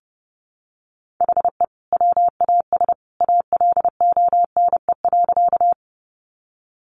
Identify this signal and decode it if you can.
Morse code, heard as a keyed tone, about 30 wpm, 700 Hz. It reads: HE WAS ALONE.
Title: Sentences from Top 1000 Words 30wpm